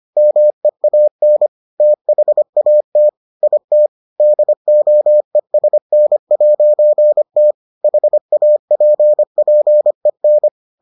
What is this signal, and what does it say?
Morse code, heard as a keyed tone, about 25 wpm, 600 Hz. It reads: MEAN THAT IT DOESN'T HAPPEN